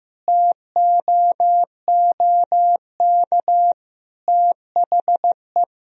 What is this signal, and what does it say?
Morse code, heard as a keyed tone, about 16 wpm, 700 Hz. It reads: TOOK THE
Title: Sentences from Top 1000 Words - Review Part 2 - 15wpm